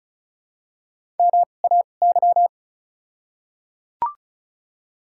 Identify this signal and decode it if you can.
Morse code, heard as a keyed tone, about 34 wpm, spaced as for 25 wpm, 700 Hz. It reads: MAY
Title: 2x Character Spacing - Top 200 Words - 35wpm